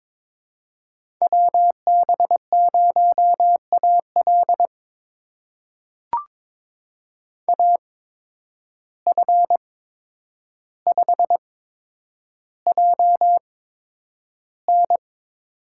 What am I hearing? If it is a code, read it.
Morse code, heard as a keyed tone, about 21 wpm, spaced as for 9 wpm, 700 Hz. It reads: WB0AL AF5JN
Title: USA Call Signs - 5 Characters - 8x Character Spacing - 22wpm